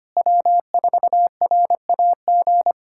Morse code, heard as a keyed tone, about 25 wpm, 700 Hz. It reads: W4RAG
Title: USA Callsign DE Callsign - 5 Characters - 25wpm